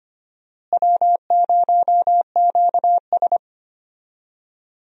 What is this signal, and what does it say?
Morse code, heard as a keyed tone, about 25 wpm, 700 Hz. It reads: W0QS
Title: USA Call Signs - 4 Characters - Repeated 3x - 25wpm